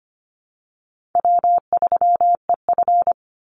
Morse code, heard as a keyed tone, about 25 wpm, 700 Hz. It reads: W3EF
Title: Easy Call Signs - 25wpm